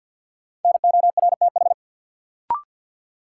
Morse code, heard as a keyed tone, about 53 wpm, 700 Hz. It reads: NORTH
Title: Sentences from Top 400 Words - New Word Drill - 50wpm